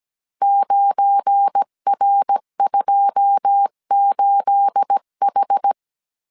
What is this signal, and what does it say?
Morse code, heard as a keyed tone, about 17 wpm, 800 Hz. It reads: 9R28H
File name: CW / DET / 1 9 1500C S B 16wpm 2856